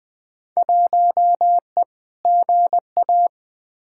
Morse code, heard as a keyed tone, about 20 wpm, 700 Hz. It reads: 1E GA
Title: Field Day Exchanges - 20wpm